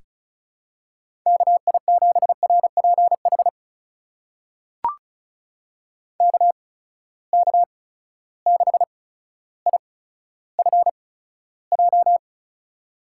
Morse code, heard as a keyed tone, about 36 wpm, spaced as for 15 wpm, 700 Hz. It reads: KI7RPH KK6IFJ
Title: USA Call Signs - 6 Characters - 8x Character Spacing - 35wpm